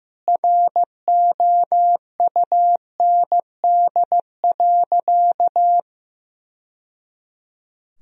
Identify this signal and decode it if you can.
Morse code, heard as a keyed tone, about 15 wpm, 700 Hz. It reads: ROUND.
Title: Sentences from Top 1000 Words - Review Part 1 - 15wpm